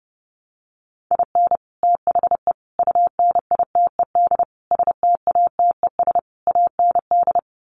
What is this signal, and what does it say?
Morse code, heard as a keyed tone, about 30 wpm, 700 Hz. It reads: IN THE UNITED STATES AND
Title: Sets of 5 Words 30wpm